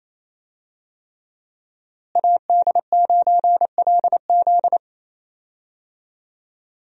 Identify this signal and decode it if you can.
Morse code, heard as a keyed tone, about 27 wpm, 700 Hz. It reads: AD9LZ
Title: USA Call Signs - 5 Characters - Repeated 3x - 28wpm